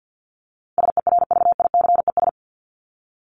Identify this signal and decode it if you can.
Morse code, heard as a keyed tone, about 50 wpm, 700 Hz. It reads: SERVICES
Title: ES:S Suffix - Top 100 Words with Base Words 50wpm